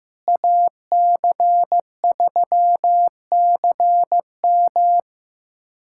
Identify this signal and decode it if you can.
Morse code, heard as a keyed tone, about 16 wpm, 700 Hz. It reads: AC3CM
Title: USA Callsign DE Callsign - 5 Characters - 15wpm